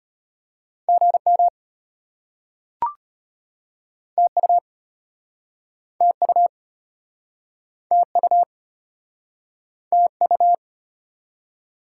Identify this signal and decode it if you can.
Morse code, heard as a keyed tone, about 37 wpm, 700 Hz. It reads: GM TU TU TU TU